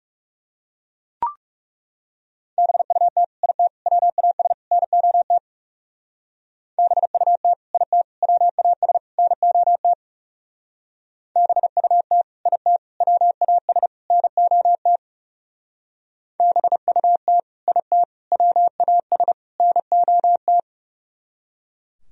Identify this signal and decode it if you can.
Morse code, heard as a keyed tone, about 38 wpm, 700 Hz. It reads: BUT IT WAS NOT BUT IT WAS NOT BUT IT WAS NOT BUT IT WAS NOT